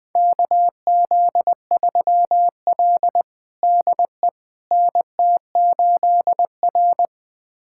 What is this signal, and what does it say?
Morse code, heard as a keyed tone, about 20 wpm, 700 Hz. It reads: KZ3L DE NT8R
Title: USA Callsign DE Callsign - 4 Characters - 20wpm